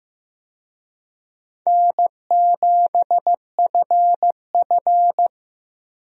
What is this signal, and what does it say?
Morse code, heard as a keyed tone, about 15 wpm, 700 Hz. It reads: N7FF
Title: USA Call Signs - 4 Characters - Repeated 2x - 15wpm